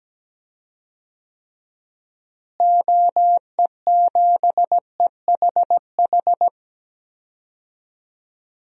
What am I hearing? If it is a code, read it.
Morse code, heard as a keyed tone, about 17 wpm, 700 Hz. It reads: OE7EHH
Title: DXCC-100 Call Signs - 6 Characters - Repeated 3x - 17wpm